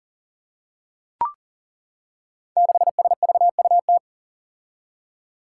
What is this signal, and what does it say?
Morse code, heard as a keyed tone, about 37 wpm, 700 Hz. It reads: 6SVUT
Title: Five Challenging Letter-Number Combinations 40wpm